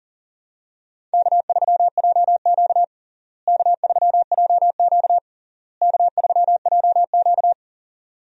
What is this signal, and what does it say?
Morse code, heard as a keyed tone, about 40 wpm, 700 Hz. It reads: K3JQ K3JQ K3JQ
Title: USA Call Signs - 4 Characters - Repeated 3x - 40wpm